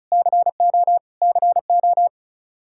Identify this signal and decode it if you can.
Morse code, heard as a keyed tone, about 34 wpm, 700 Hz. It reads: CO CO
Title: Practice QSO - Exchange 1 - Beginning - 35wpm